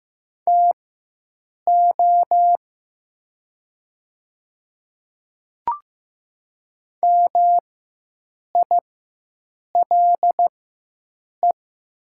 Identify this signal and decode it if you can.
Morse code, heard as a keyed tone, about 15 wpm, spaced as for 7 wpm, 700 Hz. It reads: TO MILE
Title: 4x Character Spacing - Top 300 Words - 15wpm